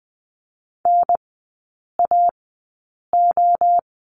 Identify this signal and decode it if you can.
Morse code, heard as a keyed tone, about 20 wpm, spaced as for 8 wpm, 700 Hz. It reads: NAO
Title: 035 Lesson 7e - Letter s - Three Characters 20wpm